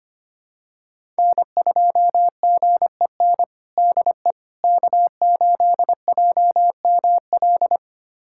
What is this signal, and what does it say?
Morse code, heard as a keyed tone, about 25 wpm, 700 Hz. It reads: N2GEN DE K8JML
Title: USA Callsign DE Callsign - 5 Characters - 25wpm